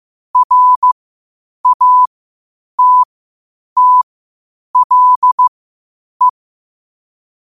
Morse code, heard as a keyed tone, about 15 wpm, spaced as for 8 wpm, 1000 Hz. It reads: RATTLE
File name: qotdfiles / 8wpm